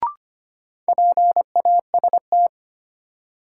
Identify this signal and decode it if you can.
Morse code, heard as a keyed tone, about 25 wpm, 700 Hz. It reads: PAST
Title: Sets of 2 Words - New Word Drill - 25wpm